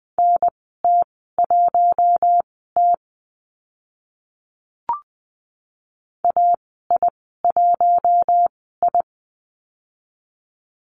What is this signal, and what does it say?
Morse code, heard as a keyed tone, about 20 wpm, spaced as for 14 wpm, 700 Hz. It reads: NT1T AI1I
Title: 042 Lesson 8g - Number 1 - Call Signs 20wpm